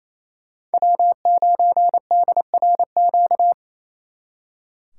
Morse code, heard as a keyed tone, about 33 wpm, 700 Hz. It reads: W9DRQ
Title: Speed Racing - USA 1x3 Call Signs 28wpm